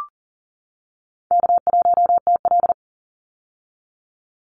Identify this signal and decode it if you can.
Morse code, heard as a keyed tone, about 40 wpm, 700 Hz. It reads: K1TL
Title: Easy Call Signs - 40wpm